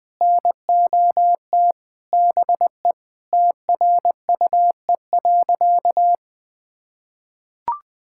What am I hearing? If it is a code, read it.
Morse code, heard as a keyed tone, about 21 wpm, 700 Hz. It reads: NOT BE TRUE.
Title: Sentences with 3-Word Phrases - 20wpm